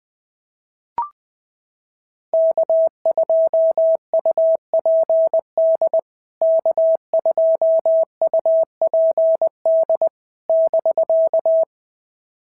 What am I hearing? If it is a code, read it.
Morse code, heard as a keyed tone, about 20 wpm, 650 Hz. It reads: K2UPD K2UPD <BK>
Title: POTA QSO Practice - 20wpm